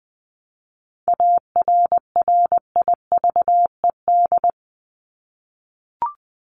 Morse code, heard as a keyed tone, about 20 wpm, 700 Hz. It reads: ARRIVED
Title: ED Suffix - Top 100 Words with Base Words 20wpm